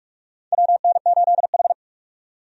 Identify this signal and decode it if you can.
Morse code, heard as a keyed tone, about 46 wpm, 700 Hz. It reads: WN8H
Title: USA Call Signs - 4 Characters - 45wpm